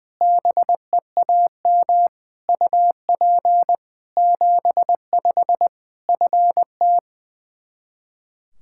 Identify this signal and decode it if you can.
Morse code, heard as a keyed tone, about 21 wpm, 700 Hz. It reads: BEAM UP 75 FT
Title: K1USN SST QSO Practice 20wpm